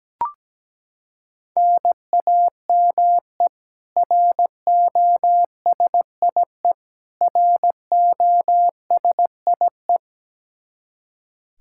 Morse code, heard as a keyed tone, about 17 wpm, 700 Hz. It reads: NAME ROSIE ROSIE
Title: K1USN SST QSO Practice 17wpm